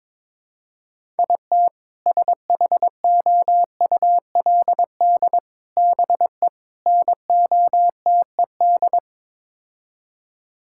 Morse code, heard as a keyed tone, about 22 wpm, 700 Hz. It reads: IT SHOULD BE NOTED
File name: practice / 22wpm / Build A Phrase - 5-Word Phrases - 22wpm